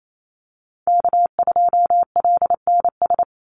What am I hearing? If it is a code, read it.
Morse code, heard as a keyed tone, about 28 wpm, 700 Hz. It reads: K2LNS
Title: Call Signs - 28wpm